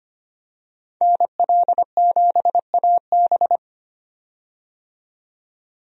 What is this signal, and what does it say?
Morse code, heard as a keyed tone, about 25 wpm, 700 Hz. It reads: NL7AB
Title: USA Call Signs - 5 Characters - 25wpm